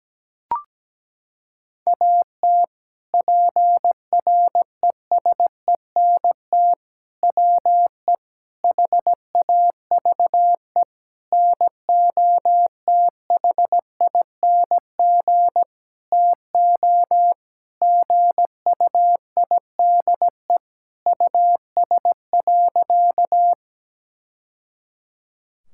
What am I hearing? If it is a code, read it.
Morse code, heard as a keyed tone, about 17 wpm, 700 Hz. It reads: AT PRESENT WE HAVE NOTHING TO GUIDE US.
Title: Sentences from Top 1000 Words - Review Part 5 - 17wpm